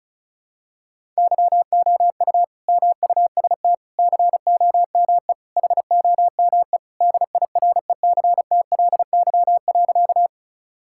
Morse code, heard as a keyed tone, about 36 wpm, 700 Hz. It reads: YOU MUST COME HOME DIRECTLY.
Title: Sentences from Top 1000 Words - Review Part 5 - 35wpm